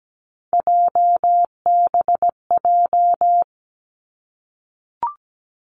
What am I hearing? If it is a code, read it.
Morse code, heard as a keyed tone, about 17 wpm, 700 Hz. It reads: JBJ
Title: Three Challenging Letter-Number Combinations 17wpm